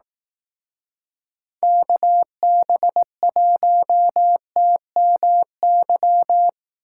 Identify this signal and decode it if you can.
Morse code, heard as a keyed tone, about 18 wpm, 700 Hz. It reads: KB1TMY
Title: Speed Racing - USA 2x3 Call Signs 15wpm